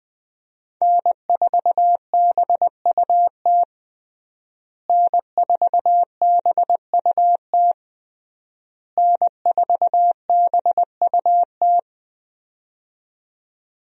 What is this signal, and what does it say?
Morse code, heard as a keyed tone, about 20 wpm, 700 Hz. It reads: N4BUT N4BUT N4BUT
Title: USA Call Signs - 5 Characters - Repeated 3x - 20wpm